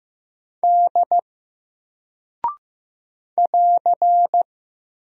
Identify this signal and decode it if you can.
Morse code, heard as a keyed tone, about 15 wpm, 700 Hz. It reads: D <AR>
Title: Single Character (Letter:Number:Punctuation:Prosign) Alt 15wpm